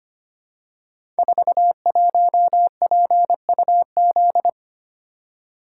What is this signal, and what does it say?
Morse code, heard as a keyed tone, about 25 wpm, 700 Hz. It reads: 41PUZ
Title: Code Groups - 5 Letter:Numbers - 25wpm